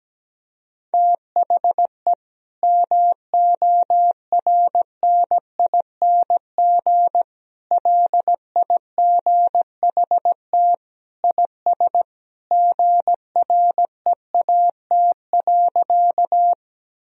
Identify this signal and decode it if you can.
Morse code, heard as a keyed tone, about 17 wpm, 700 Hz. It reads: THE MORNING LIGHT IS GREAT.
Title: Sentences from Top 1000 Words - Review Part 1 - 17wpm